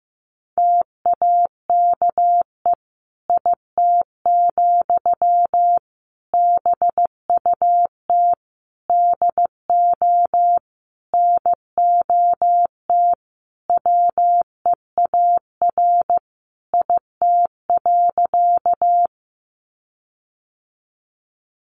Morse code, heard as a keyed tone, about 15 wpm, 700 Hz. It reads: TAKE IT, BUT DO NOT WEAR IT.
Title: Sentences from Top 1500 Words 15wpm